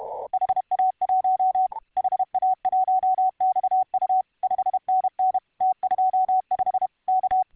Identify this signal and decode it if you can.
Morse code, heard as a keyed tone, about 31 wpm, 750 Hz. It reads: HA1 HA1XU 5NN T25 K